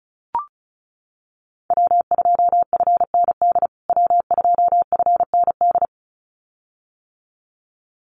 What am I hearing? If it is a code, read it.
Morse code, heard as a keyed tone, about 34 wpm, 700 Hz. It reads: W2FND W2FND
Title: USA Call Signs Repeated 2x 35wpm